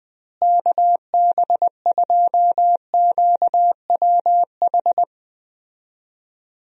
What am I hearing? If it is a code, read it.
Morse code, heard as a keyed tone, about 20 wpm, 700 Hz. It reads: KB2QWH
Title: USA Call Signs - 6 Characters - Repeated 3x - 20wpm